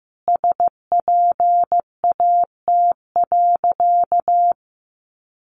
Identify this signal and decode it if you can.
Morse code, heard as a keyed tone, about 15 wpm, 700 Hz. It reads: SPAT.
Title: A Sound Of Thunder 15wpm